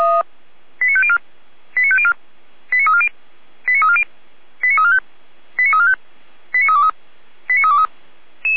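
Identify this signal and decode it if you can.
Five-tone selective call (ZVEI), heard as quick runs of five tones, tones of about 70 ms. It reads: 89504 89504 89340 89340 89345 89345 89323 89323
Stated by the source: ZVEI five-tone call 89305 received by SDR on channel 455.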